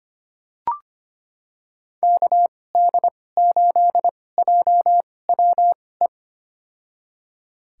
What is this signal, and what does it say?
Morse code, heard as a keyed tone, about 25 wpm, spaced as for 18 wpm, 700 Hz. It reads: KD8JWE
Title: USA Call Signs - 2x Character Spacing - 25wpm